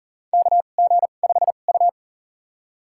Morse code, heard as a keyed tone, about 40 wpm, spaced as for 29 wpm, 700 Hz. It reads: KG5U
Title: USA Call Signs - 4 Characters - 2x Character Spacing - 40wpm